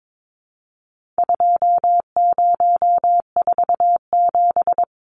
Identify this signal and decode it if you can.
Morse code, heard as a keyed tone, about 22 wpm, 700 Hz. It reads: 2047
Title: Numbers - 4-Digit 22wpm